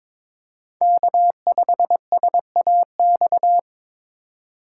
Morse code, heard as a keyed tone, about 22 wpm, 700 Hz. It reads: K5SAX
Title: USA Call Signs - 5 Characters - 4x Character Spacing - 22wpm